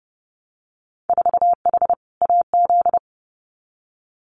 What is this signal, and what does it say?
Morse code, heard as a keyed tone, about 30 wpm, 700 Hz. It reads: 4H AZ
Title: Winter Field Day Exchanges Encore 30wpm